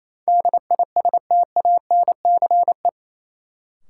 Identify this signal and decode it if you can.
Morse code, heard as a keyed tone, about 28 wpm, 700 Hz. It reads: DISTANCE
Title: Sentences from Top 1200 Words - New Word Drill Alt - 28wpm